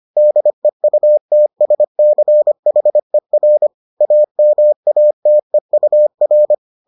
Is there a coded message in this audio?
Yes: Morse code from 0.2 to 6.5 s, heard as a keyed tone, about 25 wpm, 600 Hz. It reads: DEUTSCHER AMATEUR